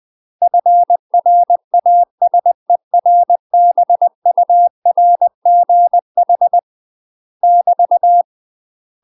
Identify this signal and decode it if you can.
Morse code, heard as a keyed tone, about 20 wpm, 700 Hz. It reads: FRASERBURGH <BT>
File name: UK MASTER QSOs TRAINING / 20WPM / UK MASTER QSOs 0071